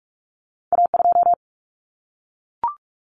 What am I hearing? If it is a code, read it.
Morse code, heard as a keyed tone, about 46 wpm, 700 Hz. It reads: A2
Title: Common Cut Numbers - 2-Digit 45wpm